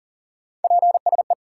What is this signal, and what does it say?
Morse code, heard as a keyed tone, about 52 wpm, 700 Hz. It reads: PSE